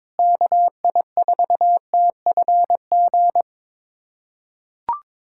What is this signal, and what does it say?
Morse code, heard as a keyed tone, about 22 wpm, 700 Hz. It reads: KI4TFG
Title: USA Callsign DE Callsign - 6 Characters - 22wpm